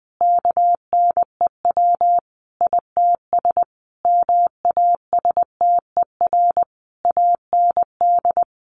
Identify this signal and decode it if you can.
Morse code, heard as a keyed tone, about 20 wpm, 700 Hz. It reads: KNEW ITS MASTER AND
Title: Sentences from Top 1300 Words 20wpm